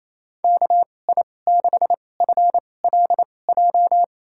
Morse code, heard as a keyed tone, about 28 wpm, spaced as for 20 wpm, 700 Hz. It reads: KI6FLJ
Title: USA Call Signs - 6 Characters - 2x Character Spacing - 28wpm